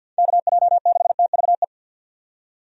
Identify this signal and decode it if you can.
Morse code, heard as a keyed tone, about 51 wpm, 700 Hz. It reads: KJ6TVE
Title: USA Call Signs - 6 Characters - 50wpm